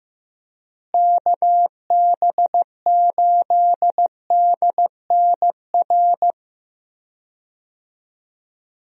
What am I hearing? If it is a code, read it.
Morse code, heard as a keyed tone, about 15 wpm, 700 Hz. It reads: KB8DNR
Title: USA Call Signs - 6 Characters - Repeated 2x - 15wpm